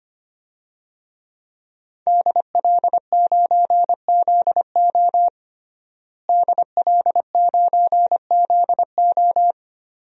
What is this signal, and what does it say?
Morse code, heard as a keyed tone, about 25 wpm, 700 Hz. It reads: DL9ZO DL9ZO
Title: DXCC-100 Call Signs - 5 Characters - Repeated 2x - 25wpm